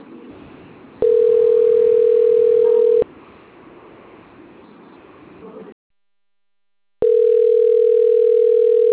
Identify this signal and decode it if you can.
Ringback tone (North America), 440 + 480 Hz, with speech.
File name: out- - - - - .14394